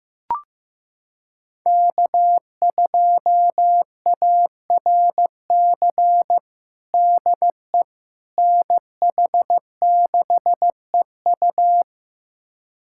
Morse code, heard as a keyed tone, about 15 wpm, 700 Hz. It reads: K2ARC DE NH6EU
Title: USA Callsign DE Callsign - 5 Characters - 15wpm